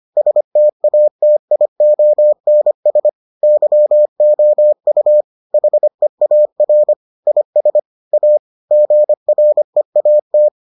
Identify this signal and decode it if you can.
Morse code, heard as a keyed tone, about 25 wpm, 600 Hz. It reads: STATIONS YOU HEAR IS A GREAT